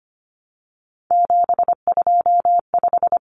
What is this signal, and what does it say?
Morse code, heard as a keyed tone, about 25 wpm, 700 Hz. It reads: 725
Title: Common Cut Numbers - 3-Digit 25wpm